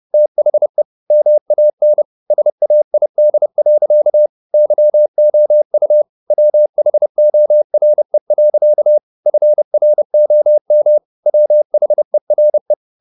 Morse code, heard as a keyed tone, about 30 wpm, 600 Hz. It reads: THE MAN SAID. YOU WHORE. FROM WHERE
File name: the old man and the sea / the old man and the sea 1 30wpm